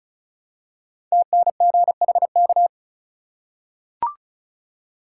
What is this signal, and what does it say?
Morse code, heard as a keyed tone, about 36 wpm, 700 Hz. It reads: TNGHK
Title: Code Groups - 5 Letter:Numbers - 35wpm